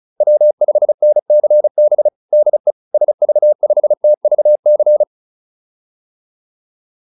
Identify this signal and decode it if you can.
Morse code, heard as a keyed tone, about 34 wpm, 600 Hz. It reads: W5NCB DE SV5TVC